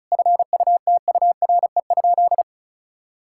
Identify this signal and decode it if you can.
Morse code, heard as a keyed tone, about 34 wpm, 700 Hz. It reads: FUTURE?
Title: Sentences from Top 1000 Words - Review Part 6 - 35wpm